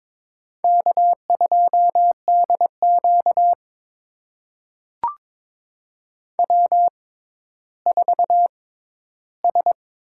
Morse code, heard as a keyed tone, about 21 wpm, spaced as for 12 wpm, 700 Hz. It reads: K2DQ W4S
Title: USA Call Signs - 6x Character Spacing - 22wpm